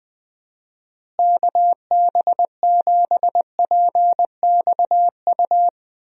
Morse code, heard as a keyed tone, about 21 wpm, 700 Hz. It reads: KB7PXU
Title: USA Call Signs - 6 Characters - Repeated 3x - 20wpm